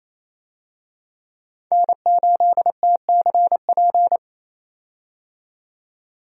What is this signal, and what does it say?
Morse code, heard as a keyed tone, about 28 wpm, 700 Hz. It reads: N8TCP
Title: USA Call Signs - 5 Characters - 28wpm